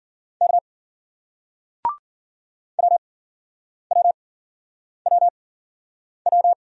Morse code, heard as a keyed tone, about 56 wpm, spaced as for 9 wpm, 700 Hz. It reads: K WWWW